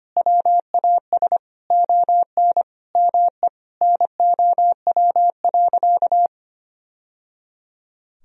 Morse code, heard as a keyed tone, about 25 wpm, 700 Hz. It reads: WAS ON ME NOW.
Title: Sentences from Top 1000 Words - Review Part 3 - 25wpm